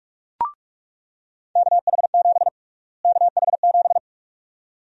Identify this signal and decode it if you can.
Morse code, heard as a keyed tone, about 46 wpm, 700 Hz. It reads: KH7 KH7
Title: USA Call Sign Prefixes - Repeated 2x - 45wpm